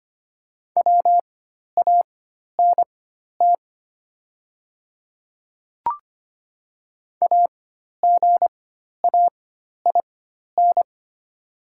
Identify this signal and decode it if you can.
Morse code, heard as a keyed tone, about 25 wpm, spaced as for 12 wpm, 700 Hz. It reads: WANT AGAIN
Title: 4x Character Spacing - Top 300 Words - 25wpm